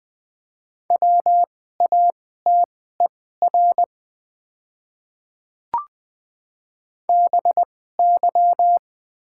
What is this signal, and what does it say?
Morse code, heard as a keyed tone, about 20 wpm, spaced as for 15 wpm, 690 Hz. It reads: WATER BY
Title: 2x Character Spacing - Top 100 Words - 20wpm